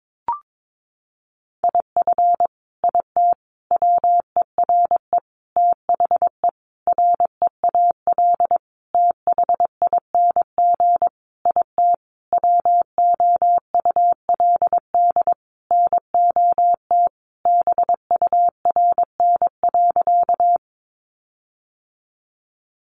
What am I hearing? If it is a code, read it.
Morse code, heard as a keyed tone, about 22 wpm, 700 Hz. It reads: IF IT WERE THE REAL THING IT WOULD NOT BURN.
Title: Sentences from Top 1300 Words 22wpm